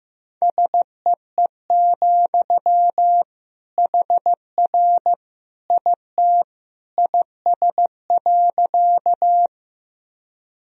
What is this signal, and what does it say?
Morse code, heard as a keyed tone, about 15 wpm, 700 Hz. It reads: SEE, HR IT IS.
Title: Sentences from Top 200 Words and Abbreviations 15wpm